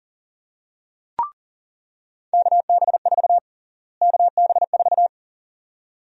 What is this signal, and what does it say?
Morse code, heard as a keyed tone, about 40 wpm, 700 Hz. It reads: KB4 KB4
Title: USA Call Sign Prefixes - Repeated 2x - 40wpm